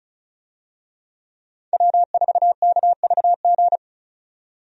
Morse code, heard as a keyed tone, about 34 wpm, 700 Hz. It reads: W4KVG
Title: USA Call Signs - 5 Characters - 35wpm